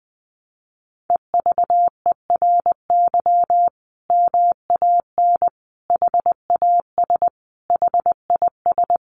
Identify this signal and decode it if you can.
Morse code, heard as a keyed tone, about 20 wpm, 700 Hz. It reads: EVERY MAN HAS HIS
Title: Sentences from Top 600 Words 20wpm